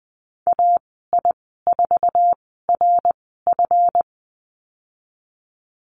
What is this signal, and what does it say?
Morse code, heard as a keyed tone, about 20 wpm, spaced as for 15 wpm, 700 Hz. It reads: AI4RF
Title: 154 Lesson 25g - Letter p - Call Signs 20wpm